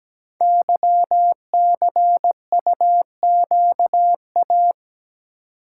Morse code, heard as a keyed tone, about 17 wpm, 700 Hz. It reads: YCUQA
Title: Code Groups - 5 Letters - 17wpm